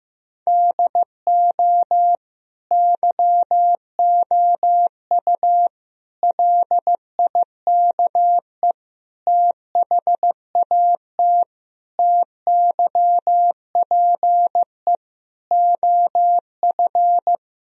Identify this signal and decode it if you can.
Morse code, heard as a keyed tone, about 15 wpm, 700 Hz. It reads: DO YOU LIKE THAT TYPE OF
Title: Sentences from Top 1000 Words - Review Part 3 - 15wpm